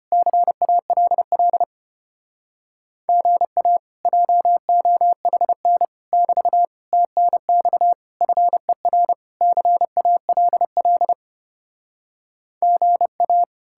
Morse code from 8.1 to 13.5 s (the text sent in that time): FER CALL GA